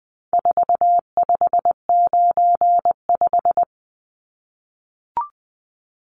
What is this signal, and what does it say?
Morse code, heard as a keyed tone, about 20 wpm, 700 Hz. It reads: 4595
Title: Numbers - 4-Digit 20wpm